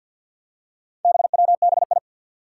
Morse code, heard as a keyed tone, about 51 wpm, 700 Hz. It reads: BWBI
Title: Four Challenging Letter-Number Combinations 50wpm